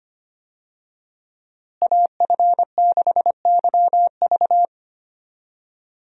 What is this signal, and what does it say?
Morse code, heard as a keyed tone, about 25 wpm, 690 Hz. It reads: AF6YV
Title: USA Call Signs - 5 Characters - Repeated 2x - 25wpm